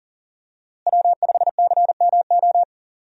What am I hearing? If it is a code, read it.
Morse code, heard as a keyed tone, about 40 wpm, 700 Hz. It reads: W5CMO
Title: USA Call Signs - 5 Characters - Repeated 2x - 40wpm